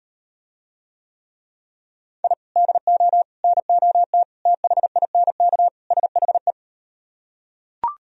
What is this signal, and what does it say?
Morse code, heard as a keyed tone, about 39 wpm, 700 Hz. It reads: I DO NOT THINK SHE